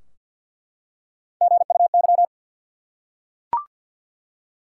Morse code, heard as a keyed tone, about 48 wpm, 700 Hz. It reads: GUY
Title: Three Letter Words - 50wpm